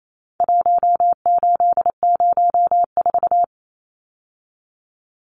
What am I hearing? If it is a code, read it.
Morse code, heard as a keyed tone, about 28 wpm, 700 Hz. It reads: 1804
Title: Numbers - Contest Style - 4-Digit 28wpm